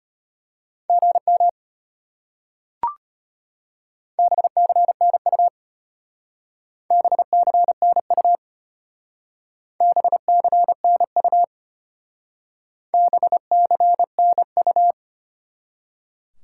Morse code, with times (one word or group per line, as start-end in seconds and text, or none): 0.9-1.5 GM
4.2-5.5 BCNU
6.9-8.4 BCNU
9.8-11.4 BCNU
12.9-14.9 BCNU